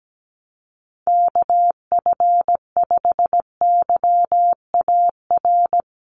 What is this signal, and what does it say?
Morse code, heard as a keyed tone, about 17 wpm, 700 Hz. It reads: KF5YAR
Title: USA Call Signs Repeated 3x 17wpm